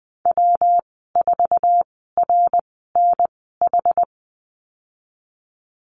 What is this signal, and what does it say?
Morse code, heard as a keyed tone, about 20 wpm, spaced as for 15 wpm, 700 Hz. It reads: W4RNH
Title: 122 Lesson 20g - Number 3 - Call Signs 20wpm